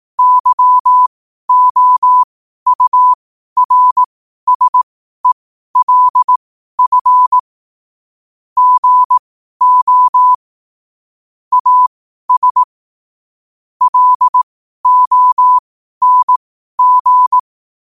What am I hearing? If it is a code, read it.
Morse code, heard as a keyed tone, about 18 wpm, spaced as for 12 wpm, 1000 Hz. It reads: YOURSELF GO AS LONG
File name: qotdfiles / 12wpm